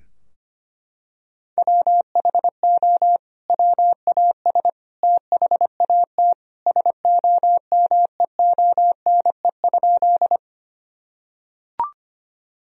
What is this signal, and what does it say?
Morse code, heard as a keyed tone, about 25 wpm, 700 Hz. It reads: WHO WAS THAT SOMEONE?
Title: Sentences from Top 1000 Words - Review Part 4 - 25wpm